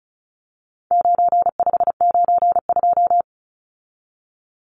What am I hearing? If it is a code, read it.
Morse code, heard as a keyed tone, about 34 wpm, 700 Hz. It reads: 9592
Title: Numbers - 4-Digit 35wpm